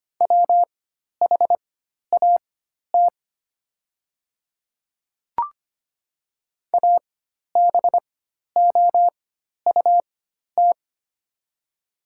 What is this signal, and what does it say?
Morse code, heard as a keyed tone, about 25 wpm, spaced as for 12 wpm, 700 Hz. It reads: WHAT ABOUT
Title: 4x Character Spacing - Top 100 Words - 25wpm